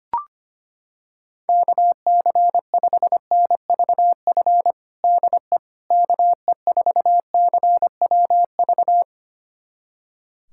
Morse code, heard as a keyed tone, about 25 wpm, 700 Hz. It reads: KC5NVF DE KE4CWV
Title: USA Callsign DE Callsign 25wpm